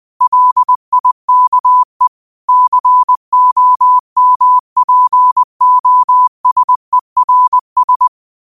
Morse code, heard as a keyed tone, about 20 wpm, 1000 Hz. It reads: LIKE COMPOSERS
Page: Quotes for Tue, 12 Aug 2025 in Morse Code at 20 words per minute.